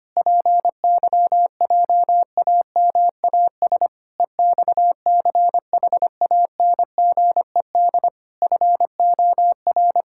Morse code, heard as a keyed tone, about 25 wpm, 700 Hz. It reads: PYJAMAS EXCHANGED FOR
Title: Pangram Challenge v2 - 25wpm